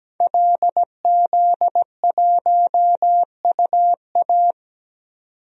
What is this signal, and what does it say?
Morse code, heard as a keyed tone, about 17 wpm, 700 Hz. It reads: LZ1UA
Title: DXCC-100 Call Signs - 5 Characters - 17wpm